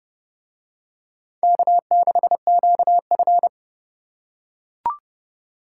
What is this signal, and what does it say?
Morse code, heard as a keyed tone, about 28 wpm, 700 Hz. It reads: K6QF